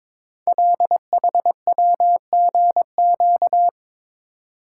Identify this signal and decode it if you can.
Morse code, heard as a keyed tone, about 23 wpm, 700 Hz. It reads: LHWGQ
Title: Code Groups - 5 Letters - 22wpm